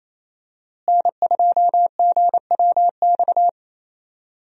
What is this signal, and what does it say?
Morse code, heard as a keyed tone, about 28 wpm, 700 Hz. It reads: N2GWX
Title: USA Call Signs - 5 Characters - Repeated 2x - 28wpm